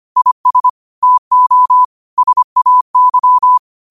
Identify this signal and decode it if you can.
Morse code, heard as a keyed tone, about 25 wpm, 1000 Hz. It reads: IS TO SAY
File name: qotdfiles / 25wpm